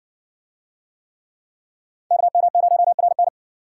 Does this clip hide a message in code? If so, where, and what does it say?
Yes: Morse code from 2.1 to 3.3 s, heard as a keyed tone, about 53 wpm, 700 Hz. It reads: KM0RN